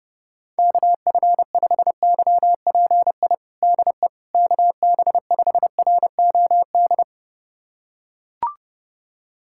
Morse code, heard as a keyed tone, about 32 wpm, 700 Hz. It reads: KF5YPI DE KB5ROD
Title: USA Callsign DE Callsign 30wpm